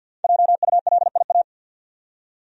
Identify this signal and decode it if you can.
Morse code, heard as a keyed tone, about 50 wpm, 700 Hz. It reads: JULIA
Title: USA Common Names - Max 5 Letters - 50wpm